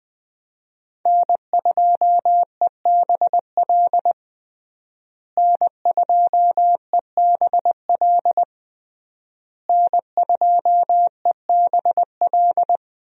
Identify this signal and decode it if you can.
Morse code, heard as a keyed tone, about 21 wpm, 700 Hz. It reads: N2EBL N2EBL N2EBL
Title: USA Call Signs - 5 Characters - Repeated 3x - 20wpm